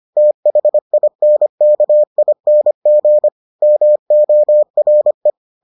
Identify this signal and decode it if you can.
Morse code, heard as a keyed tone, about 25 wpm, 600 Hz. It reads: THINKING MORE